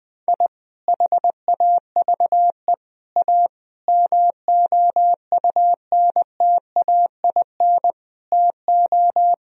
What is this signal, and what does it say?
Morse code, heard as a keyed tone, about 20 wpm, 700 Hz. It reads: I HAVE A MOUNTAIN TO
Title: Sentences from Top 1000 Words - Review Part 1 - 20wpm